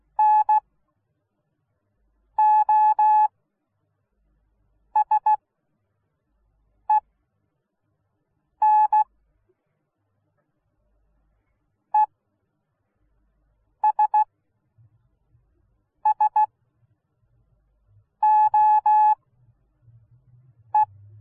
Morse code, heard as a keyed tone, about 16 wpm, spaced as for 5 wpm, 850 Hz. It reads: NOSEN ESSOE